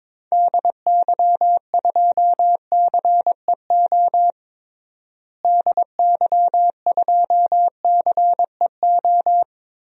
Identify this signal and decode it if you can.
Morse code, heard as a keyed tone, about 22 wpm, 700 Hz. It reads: DY2CEO DY2CEO
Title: DXCC-100 Call Signs - 6 Characters - Repeated 3x - 22wpm